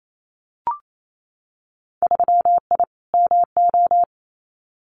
Morse code, heard as a keyed tone, about 28 wpm, 700 Hz. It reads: 3I MO
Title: Winter Field Day Exchanges 28wpm